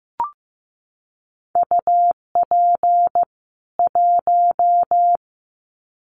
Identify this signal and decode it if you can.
Morse code, heard as a keyed tone, about 15 wpm, 700 Hz. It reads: UP 1
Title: Common QSO Phrases 15wpm